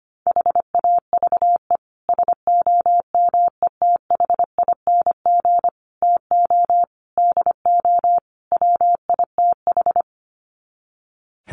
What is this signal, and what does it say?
Morse code, heard as a keyed tone, about 25 wpm, 690 Hz. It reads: HAVE SOMETHING TO DO WITH
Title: Sets of 5 Words Encore 25wpm